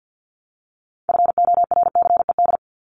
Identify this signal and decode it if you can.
Morse code, heard as a keyed tone, about 49 wpm, 700 Hz. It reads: FORCED
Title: ED Suffix - Top 100 Words with Base Words 50wpm